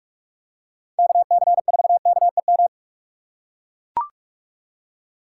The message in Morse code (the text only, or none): KC4KEM